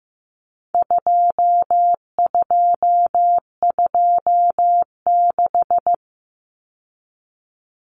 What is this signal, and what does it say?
Morse code, heard as a keyed tone, about 15 wpm, 700 Hz. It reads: 2226
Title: Common Cut Numbers - 4-Digit 15wpm